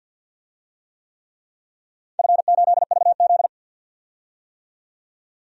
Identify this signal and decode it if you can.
Morse code, heard as a keyed tone, about 47 wpm, 700 Hz. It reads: F8VZ
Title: DXCC-100 Call Signs - 4 Characters - 50wpm